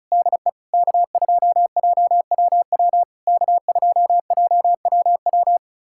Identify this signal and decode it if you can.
Morse code, heard as a keyed tone, about 36 wpm, 690 Hz. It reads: DE K2JWW K2JWW
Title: SOTA QSO Practice 35wpm